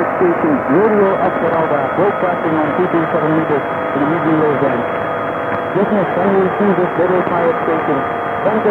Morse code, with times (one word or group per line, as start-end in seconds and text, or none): none